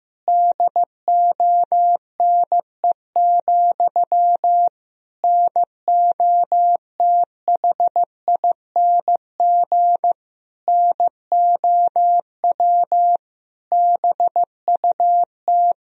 Morse code, heard as a keyed tone, about 15 wpm, 700 Hz. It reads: DONE, NOTHING NOW BUT
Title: Sentences from Top 1000 Words - Review Part 4 - 15wpm